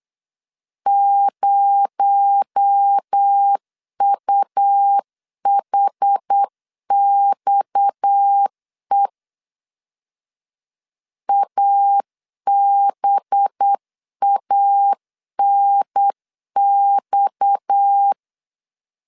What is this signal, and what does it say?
Morse code, heard as a keyed tone, about 8 wpm, 800 Hz. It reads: 0UHXE ABANX
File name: CW / DET / 1 6 1500C S C 8wpm 5219